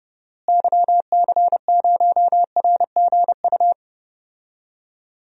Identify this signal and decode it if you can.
Morse code, heard as a keyed tone, about 30 wpm, 700 Hz. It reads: YC0RGU
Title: DXCC-100 Call Signs - 6 Characters - Repeated 2x - 30wpm